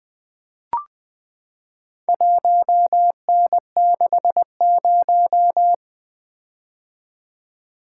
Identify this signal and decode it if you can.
Morse code, heard as a keyed tone, about 20 wpm, 700 Hz. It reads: 1N60
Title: Four Letter-Number Combinations - 20wpm